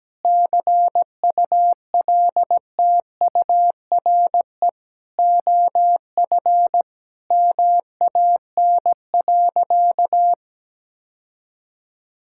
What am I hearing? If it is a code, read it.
Morse code, heard as a keyed tone, about 17 wpm, 700 Hz. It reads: CULTURE OF MAN.
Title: Sentences from Top 1000 Words - Review Part 6 - 17wpm